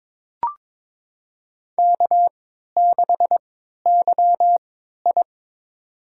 Morse code, heard as a keyed tone, about 22 wpm, spaced as for 13 wpm, 700 Hz. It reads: K6YI
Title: USA Call Signs - 4 Characters - 3x Character Spacing - 22wpm